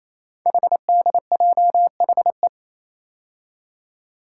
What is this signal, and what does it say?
Morse code, heard as a keyed tone, about 28 wpm, 690 Hz. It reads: HDJHE
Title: Code Groups - 5 Letters - 28wpm